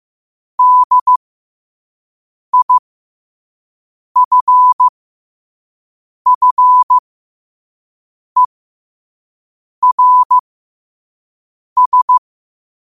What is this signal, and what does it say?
Morse code, heard as a keyed tone, about 15 wpm, spaced as for 5 wpm, 1000 Hz. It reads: DIFFERS
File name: 5wpm